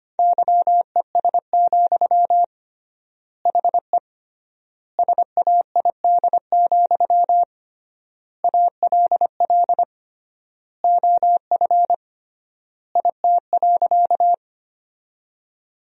Morse code, heard as a keyed tone, about 25 wpm, 700 Hz. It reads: YES, HE SAID, ALL OF IT.
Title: 3x Word-Spacing - Sentences from Top 500 Words - 25wpm